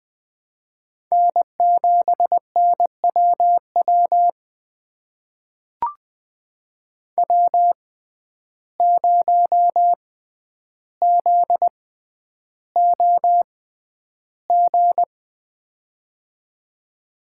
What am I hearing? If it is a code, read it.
Morse code, heard as a keyed tone, about 20 wpm, spaced as for 10 wpm, 700 Hz. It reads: N7NWW W0ZOG
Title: USA Call Signs - 5 Characters - 6x Character Spacing - 20wpm